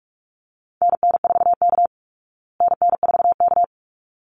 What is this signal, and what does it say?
Morse code, heard as a keyed tone, about 44 wpm, 700 Hz. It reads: NN4K NN4K
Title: Call Signs Repeated 3x - 45wpm